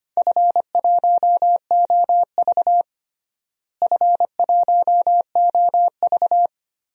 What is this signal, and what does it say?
Morse code, heard as a keyed tone, about 25 wpm, 700 Hz. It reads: F1OV F1OV
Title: DXCC-100 Call Signs - 4 Characters - Repeated 3x - 25wpm